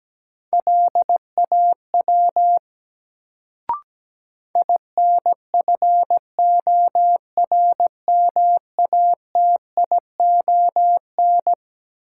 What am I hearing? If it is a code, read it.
Morse code, heard as a keyed tone, about 17 wpm, 700 Hz. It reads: LAW INFORMATION
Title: Sentences from Top 600 Words - New Word Drill - 17wpm